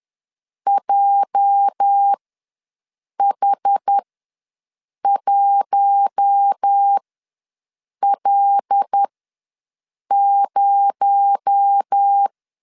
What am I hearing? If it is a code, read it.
Morse code, heard as a keyed tone, about 10 wpm, spaced as for 6 wpm, 800 Hz. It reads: JH1L0
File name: LEZIONI / 9 28 ws 10wpm 4124